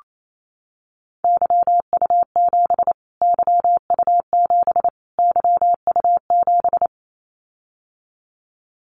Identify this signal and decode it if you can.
Morse code, heard as a keyed tone, about 28 wpm, 700 Hz. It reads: YU7 YU7 YU7
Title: Callsign Prefixes for Top 25 DXCC - Repeated 3x - 28wpm